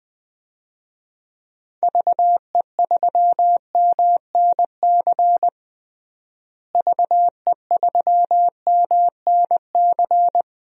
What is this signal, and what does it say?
Morse code, heard as a keyed tone, about 20 wpm, 700 Hz. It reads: VE3MNC VE3MNC
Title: DXCC-100 Call Signs - 6 Characters - Repeated 3x - 20wpm